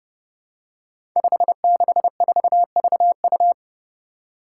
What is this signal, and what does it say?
Morse code, heard as a keyed tone, about 30 wpm, 700 Hz. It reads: 564VU
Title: Five Challenging Letter-Number Combinations 30wpm